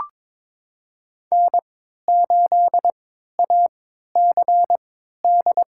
Morse code, heard as a keyed tone, about 22 wpm, spaced as for 13 wpm, 700 Hz. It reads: N8ACD
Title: USA Call Signs - 5 Characters - 3x Character Spacing - 22wpm